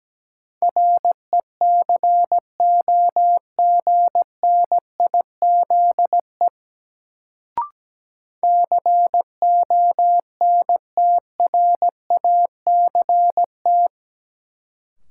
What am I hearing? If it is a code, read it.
Morse code, heard as a keyed tone, about 17 wpm, 700 Hz. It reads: RECOGNIZE CONTRACT
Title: Sentences from Top 1400 Words - New Word Drill - 17wpm